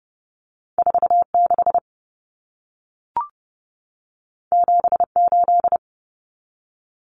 Morse code, heard as a keyed tone, about 30 wpm, 700 Hz. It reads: 46 78
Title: Numbers - 2-Digit 30wpm